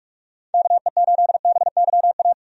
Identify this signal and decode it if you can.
Morse code, heard as a keyed tone, about 44 wpm, 700 Hz. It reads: KE8BYA
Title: USA Call Signs - 6 Characters - 45wpm